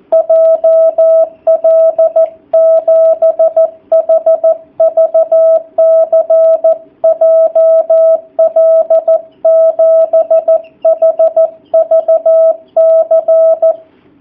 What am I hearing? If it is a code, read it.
Morse code, heard as a keyed tone, about 14 wpm, 660 Hz, with radio noise. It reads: JL7HVC JL7HVC
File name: jh8cbh / 2019cw / sp13- w